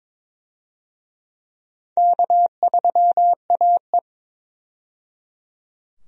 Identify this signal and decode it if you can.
Morse code, heard as a keyed tone, about 21 wpm, 700 Hz. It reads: K3AE
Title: USA Call Signs - 4 Characters - 22wpm